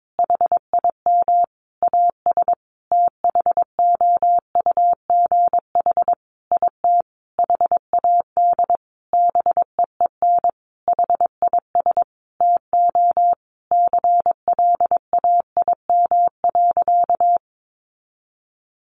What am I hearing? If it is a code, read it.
Morse code, heard as a keyed tone, about 22 wpm, 700 Hz. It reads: HIM AS THOUGH IT HAD BEEN HIS TO CLAIM.
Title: Sentences from Top 1500 Words 22wpm